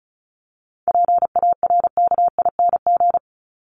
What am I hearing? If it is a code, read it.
Morse code, heard as a keyed tone, about 34 wpm, 700 Hz. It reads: PARKING
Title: Top 1501-1750 Words 35wpm